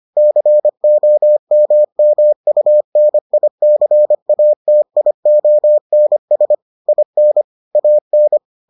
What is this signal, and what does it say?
Morse code, heard as a keyed tone, about 25 wpm, 600 Hz. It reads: COMMUNICATIONS IN AN